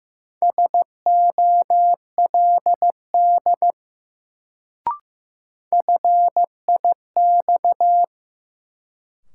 Morse code, heard as a keyed tone, about 16 wpm, 700 Hz. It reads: SOLD FIX
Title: Sentences from Top 1400 Words - New Word Drill Alt - 15wpm